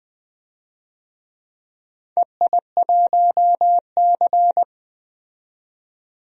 Morse code, heard as a keyed tone, about 20 wpm, 700 Hz. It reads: EI1C
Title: DXCC-100 Call Signs - 4 Characters - 20wpm